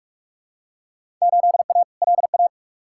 Morse code, heard as a keyed tone, about 46 wpm, 700 Hz. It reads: 8A LA
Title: Field Day Exchanges Encore - 45wpm